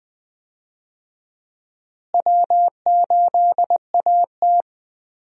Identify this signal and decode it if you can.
Morse code, heard as a keyed tone, about 20 wpm, 700 Hz. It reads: W8AT
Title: USA Call Signs - 4 Characters - Repeated 2x - 20wpm